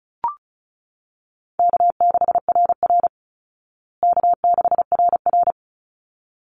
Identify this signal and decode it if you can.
Morse code, heard as a keyed tone, about 34 wpm, 700 Hz. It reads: K6RR K6RR
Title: Easy Call Signs Repeated 2x - 35wpm